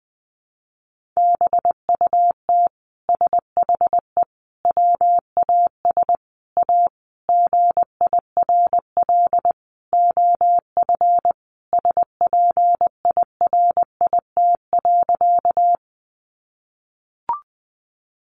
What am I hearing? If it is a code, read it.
Morse code, heard as a keyed tone, about 20 wpm, 700 Hz. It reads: BUT SHE WAS A GIRL OF SPIRIT.
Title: Sentences from Top 1400 Words 20wpm